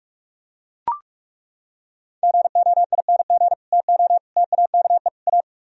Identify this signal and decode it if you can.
Morse code, heard as a keyed tone, about 38 wpm, 700 Hz. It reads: GOING TO TAKE A